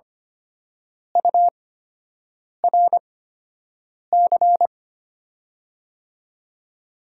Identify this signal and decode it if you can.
Morse code, heard as a keyed tone, about 25 wpm, spaced as for 7 wpm, 700 Hz. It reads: URC
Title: USA Call Signs - 8x Character Spacing - 25wpm